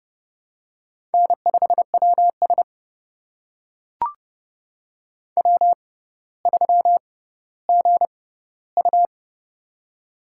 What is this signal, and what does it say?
Morse code, heard as a keyed tone, about 30 wpm, spaced as for 15 wpm, 700 Hz. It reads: N5WS W3GU
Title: USA Call Signs - 4 Characters - 6x Character Spacing - 30wpm